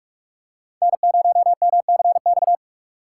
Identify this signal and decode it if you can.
Morse code, heard as a keyed tone, about 46 wpm, 700 Hz. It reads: N0MCX
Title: USA Call Signs - 5 Characters - 3x Character Spacing - 45wpm